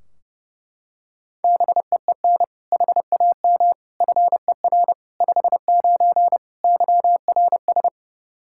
Morse code, heard as a keyed tone, about 30 wpm, 700 Hz. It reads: BEEN HAM FER 59 YRS
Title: Practice QSO - Exchange 3 - More Info - 30wpm